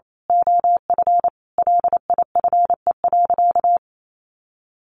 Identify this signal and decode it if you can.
Morse code, heard as a keyed tone, about 28 wpm, 700 Hz. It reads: OF LIFE.
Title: A Sound Of Thunder 28wpm